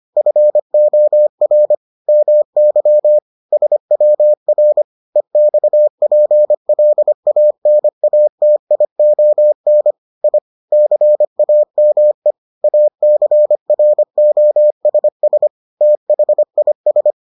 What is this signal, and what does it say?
Morse code, heard as a keyed tone, about 25 wpm, 600 Hz. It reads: FOR MY SWR EXPLANATION I CAME ACROSS THIS